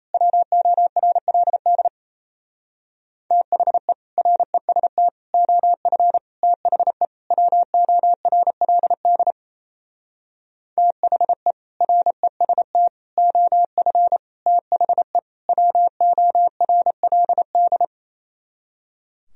Morse code, heard as a keyed tone, about 32 wpm, 700 Hz. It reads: WORLD THE REST OF THE WORLD THE REST OF THE WORLD
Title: Speed-Racing - Sets of 5 Words 28wpm